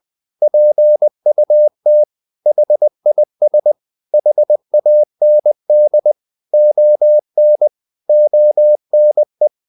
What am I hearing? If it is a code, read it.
Morse code, heard as a keyed tone, about 20 wpm, 600 Hz. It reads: PUT HIS HAND ON ONE
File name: the old man and the sea / the old man and the sea part 1 20wpm